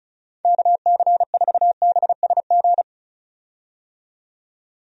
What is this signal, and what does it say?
Morse code, heard as a keyed tone, about 36 wpm, 700 Hz. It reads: KC4BSG
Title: USA Call Signs - 6 Characters - 35wpm